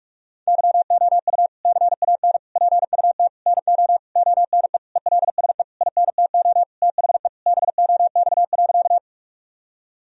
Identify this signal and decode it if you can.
Morse code, heard as a keyed tone, about 42 wpm, 700 Hz. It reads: YOU CAN PUT NO ONE ELSE INTO THE BOX.
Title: Sentences from Top 1000 Words - Review Part 2 - 45wpm